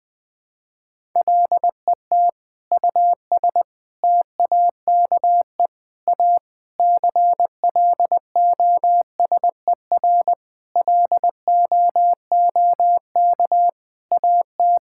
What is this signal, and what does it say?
Morse code, heard as a keyed tone, about 20 wpm, 700 Hz. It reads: LET US TAKE A CLOSER LOOK AT
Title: Sentences with 7-Word Phrases - Spoken Beginning - 20wpm